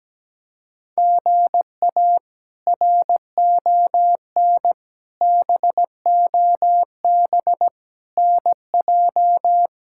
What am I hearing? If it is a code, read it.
Morse code, heard as a keyed tone, about 17 wpm, 700 Hz. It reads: GA RON BOB NJ
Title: K1USN SST QSO Practice - 17wpm